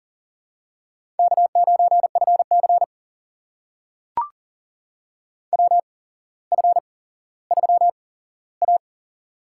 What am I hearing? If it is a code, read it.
Morse code, heard as a keyed tone, about 39 wpm, spaced as for 17 wpm, 700 Hz. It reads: K9FC WF3A
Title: USA Call Signs - 4 Characters - 8x Character Spacing - 40wpm